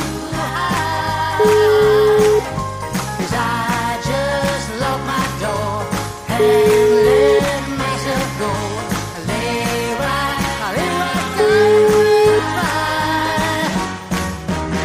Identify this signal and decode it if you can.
Ringback tone (Europe, 425 Hz), with noise.